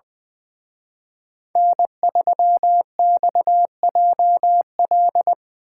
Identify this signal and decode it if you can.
Morse code, heard as a keyed tone, about 22 wpm, 700 Hz. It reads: N3XJL
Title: Speed Racing - USA 1x3 Call Signs 20wpm